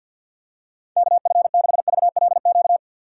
Morse code, heard as a keyed tone, about 47 wpm, 700 Hz. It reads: KF6VLQ
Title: USA Call Signs - 6 Characters - 6x Character Spacing - 50wpm